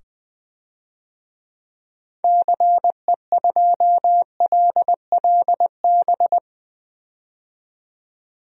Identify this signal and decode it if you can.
Morse code, heard as a keyed tone, about 20 wpm, 700 Hz. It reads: CE2LLB
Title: DXCC-100 Call Signs - 6 Characters - 20wpm